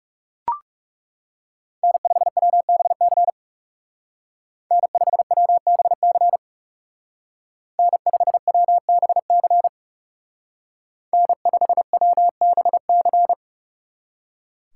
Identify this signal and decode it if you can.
Morse code, heard as a keyed tone, about 36 wpm, 700 Hz. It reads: N5WBC N5WBC N5WBC N5WBC
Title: Speed Racing - USA 1x3 Call Signs 30wpm